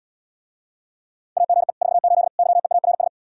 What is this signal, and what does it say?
Morse code, heard as a keyed tone, about 80 wpm, 700 Hz. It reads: AGE 48 YEARS